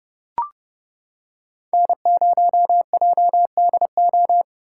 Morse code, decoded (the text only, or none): N0JDO